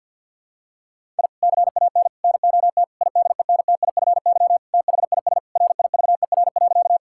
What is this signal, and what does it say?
Morse code, heard as a keyed tone, about 50 wpm, 700 Hz. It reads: I CAN NOT IDENTIFY THIS RIVER.
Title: Sentences from Top 1000 Words - Review Part 6 - 50wpm